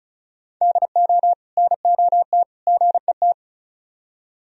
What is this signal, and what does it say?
Morse code, heard as a keyed tone, about 46 wpm, 700 Hz. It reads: DO NOT GET